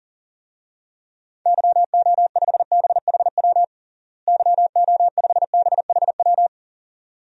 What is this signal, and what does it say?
Morse code, heard as a keyed tone, about 40 wpm, 700 Hz. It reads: YO5BHW YO5BHW
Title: DXCC-100 Call Signs - 6 Characters - Repeated 2x - 40wpm